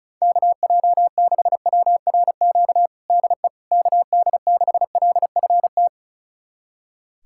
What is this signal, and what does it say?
Morse code, heard as a keyed tone, about 34 wpm, 700 Hz. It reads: KJ6WRQ DE KD6LFT
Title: USA Callsign DE Callsign - 6 Characters - 35wpm